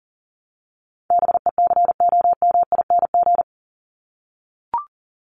Morse code, heard as a keyed tone, about 40 wpm, 700 Hz. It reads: BECOMING
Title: ING Suffix - Top 100 Words with Base Words 40wpm